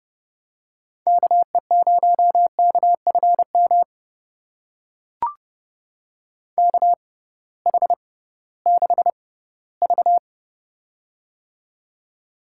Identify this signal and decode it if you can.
Morse code, heard as a keyed tone, about 32 wpm, spaced as for 18 wpm, 700 Hz. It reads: KE0KFM KH6V
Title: USA Call Signs - 6x Character Spacing - 30wpm